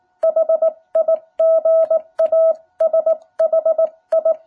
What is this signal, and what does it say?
Morse code, heard as a keyed tone, about 19 wpm, 660 Hz. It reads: HIGASHI
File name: jh8cbh / 2021cw